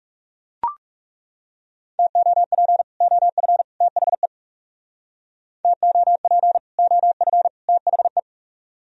Morse code, heard as a keyed tone, about 38 wpm, 700 Hz. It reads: TOP OF THE TOP OF THE
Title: Speed-Racing - Sets of 3 Words 30wpm